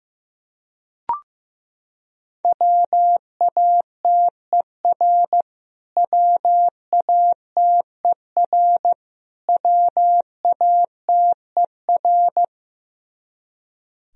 Morse code, heard as a keyed tone, about 15 wpm, 700 Hz. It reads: WATER WATER WATER
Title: Top 100 Words Repeated 15wpm